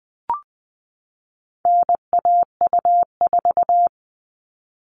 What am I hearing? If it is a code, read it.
Morse code, heard as a keyed tone, about 20 wpm, 700 Hz. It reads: NAU4
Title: All Cut Numbers - 4-Digit 20wpm